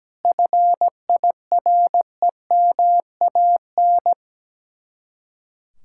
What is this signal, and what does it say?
Morse code, heard as a keyed tone, about 17 wpm, 700 Hz. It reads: FIREMAN
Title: Speed-Racing Common QSO Phrases Encore - 17wpm